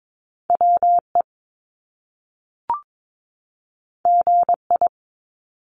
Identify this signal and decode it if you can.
Morse code, heard as a keyed tone, about 22 wpm, 700 Hz. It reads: WE GI
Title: Top Two Letter Combinations in Words 22wpm